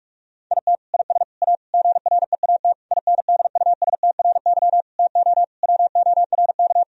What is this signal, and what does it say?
Morse code, heard as a keyed tone, about 42 wpm, 700 Hz. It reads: IT IS A GREAT INDUSTRY TO WORK
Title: Sentences from Top 1000 Words - Review Part 4 - 45wpm